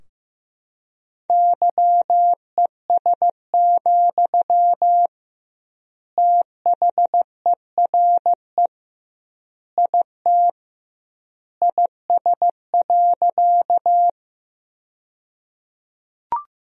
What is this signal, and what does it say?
Morse code, heard as a keyed tone, about 15 wpm, 700 Hz. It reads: YES, THERE IT IS.
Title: 2x Word-Spacing - Sentences from Top 500 Words - 15wpm